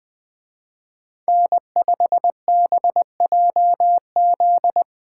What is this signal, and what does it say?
Morse code, heard as a keyed tone, about 21 wpm, 700 Hz. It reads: N5BJZ
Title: Speed Racing - USA 1x3 Call Signs 17wpm